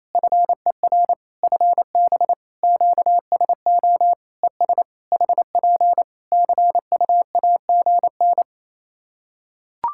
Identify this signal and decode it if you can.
Morse code, heard as a keyed tone, about 28 wpm, 700 Hz. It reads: FER FB QSO ES HP CUAGN
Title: K1USN SST QSO Practice 28wpm